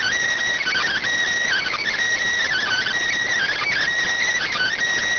Emergency Alert System (SAME) tones, heard as harsh buzzing bursts. It is heard from 0.6 to 4.7 s.